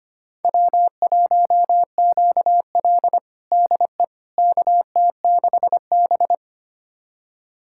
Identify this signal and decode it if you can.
Morse code, heard as a keyed tone, about 25 wpm, 700 Hz. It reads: W1QL DE KT6B
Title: USA Callsign DE Callsign - 4 Characters - 25wpm